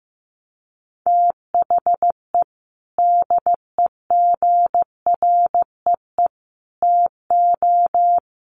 Morse code, heard as a keyed tone, about 15 wpm, 700 Hz. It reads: THE DEGREE TO
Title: Sets of 5 Words Encore2 15wpm